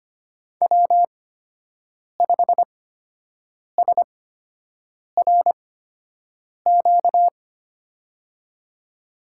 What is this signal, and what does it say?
Morse code, heard as a keyed tone, about 25 wpm, spaced as for 7 wpm, 700 Hz. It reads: W5SRQ
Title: USA Call Signs - 5 Characters - 8x Character Spacing - 25wpm